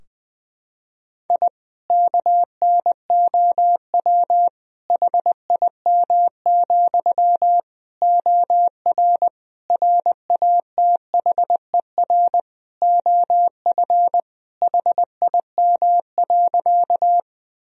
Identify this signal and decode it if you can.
Morse code, heard as a keyed tone, about 20 wpm, 700 Hz. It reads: I KNOW HIM, OR RATHER OF HIM.
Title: Sentences from Top 1000 Words - Review Part 4 - 20wpm